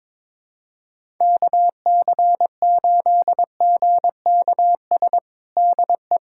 Morse code, heard as a keyed tone, about 22 wpm, 700 Hz. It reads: KC8GKS DE
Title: USA Callsign DE Callsign 22wpm